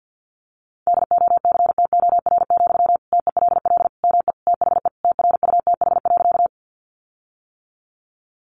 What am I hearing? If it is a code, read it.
Morse code, heard as a keyed tone, about 50 wpm, 700 Hz. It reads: DOCTOR, TELL ME THE TRUTH.
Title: Sentences from Top 1500 Words 50wpm